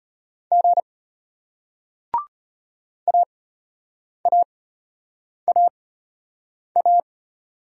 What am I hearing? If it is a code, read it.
Morse code, heard as a keyed tone, about 34 wpm, spaced as for 8 wpm, 700 Hz. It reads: G AAAA